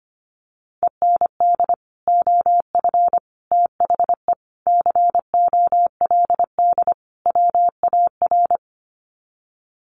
Morse code, heard as a keyed tone, about 25 wpm, 700 Hz. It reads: END OF THE COLD WAR
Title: Sets of 5 Words Encore 25wpm